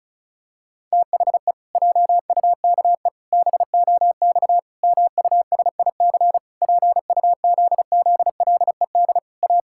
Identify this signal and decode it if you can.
Morse code, heard as a keyed tone, about 34 wpm, 700 Hz. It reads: THE JUKE BOX MUSIC PUZZLED A
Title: Pangram Challenge v2 - 35wpm